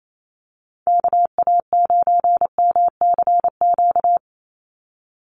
Morse code, heard as a keyed tone, about 28 wpm, 700 Hz. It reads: KA9MCQ
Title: USA Call Signs 28wpm